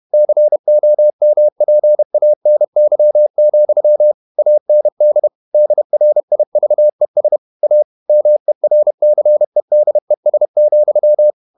Morse code, heard as a keyed tone, about 31 wpm, 600 Hz. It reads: COMPANY, AND DRIVES A MERCEDES,
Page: Never 31wpm | CW med Gnister